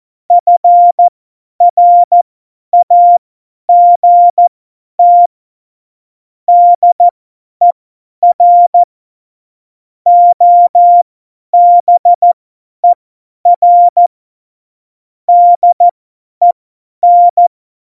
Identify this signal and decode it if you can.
Morse code, heard as a keyed tone, about 14 wpm, spaced as for 10 wpm, 700 Hz. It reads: FRAGT DER OBER DEN
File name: Morsekurs / Witze / Witz 25. Februar 2020, 50 F70